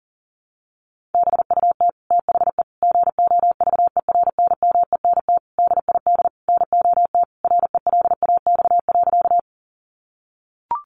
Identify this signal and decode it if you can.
Morse code, heard as a keyed tone, about 40 wpm, 700 Hz. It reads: BUT THE GOVERNMENT DID NOT RELAX.
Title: Sentences from Top 1400 Words 40wpm